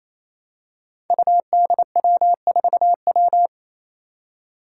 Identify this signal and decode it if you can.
Morse code, heard as a keyed tone, about 28 wpm, 700 Hz. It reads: UDW4W
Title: Five Challenging Letter-Number Combinations 28wpm